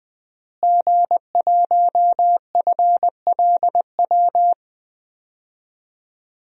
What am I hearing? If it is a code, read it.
Morse code, heard as a keyed tone, about 21 wpm, 700 Hz. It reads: G1FLW
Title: DXCC-100 Call Signs - 5 Characters - Repeated 3x - 20wpm